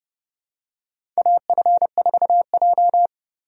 Morse code, heard as a keyed tone, about 32 wpm, 700 Hz. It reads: AF4J
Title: USA Call Signs - 4 Characters - Repeated 2x - 30wpm